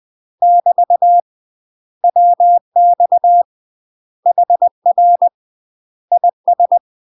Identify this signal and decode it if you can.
Morse code, heard as a keyed tone, about 20 wpm, 700 Hz. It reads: <BT> WX HR IS
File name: UK MASTER QSOs TRAINING / 20WPM / UK MASTER QSOs 0071